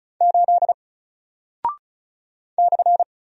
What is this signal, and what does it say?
Morse code, heard as a keyed tone, about 36 wpm, 700 Hz. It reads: 8 /
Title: Single Character (Letter:Number:Punctuation:Prosign) Alt 35wpm